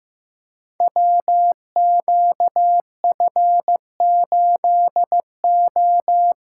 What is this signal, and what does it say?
Morse code, heard as a keyed tone, about 15 wpm, 700 Hz. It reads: WQF8O
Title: Code Groups - 5 Letter:Numbers - 15wpm